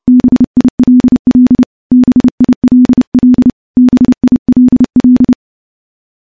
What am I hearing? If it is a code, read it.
Morse code, heard as a keyed tone, about 30 wpm, 250 Hz. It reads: BILL BILL BILL
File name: morse / morse-training-30WPM